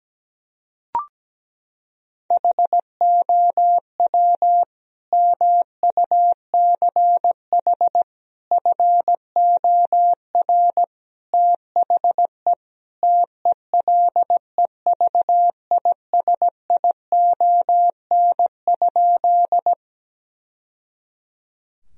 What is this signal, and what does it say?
Morse code, heard as a keyed tone, about 17 wpm, 700 Hz. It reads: HOW MUCH FOR THE TELEVISION?
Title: Sentences from Top 1000 Words - Review Part 4 - 17wpm